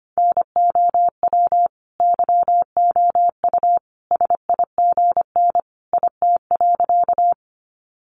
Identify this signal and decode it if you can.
Morse code, heard as a keyed tone, about 25 wpm, 700 Hz. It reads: NOW YOU SIGN IT.
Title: Sentences from Top 1000 Words 25wpm